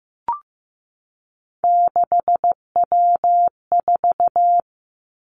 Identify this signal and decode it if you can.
Morse code, heard as a keyed tone, about 15 wpm, 700 Hz. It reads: 6W4
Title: Three Challenging Letter-Number Combinations 15wpm